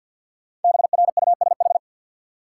Morse code, heard as a keyed tone, about 53 wpm, 700 Hz. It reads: BRUSH
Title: Sentences from Top 1100 Words - New Word Drill Alt - 50wpm